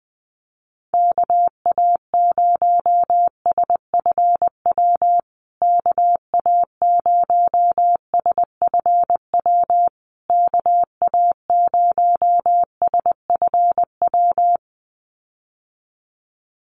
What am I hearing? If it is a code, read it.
Morse code, heard as a keyed tone, about 20 wpm, 700 Hz. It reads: KA0SFW KA0SFW KA0SFW
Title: USA Call Signs Repeated 3x 20wpm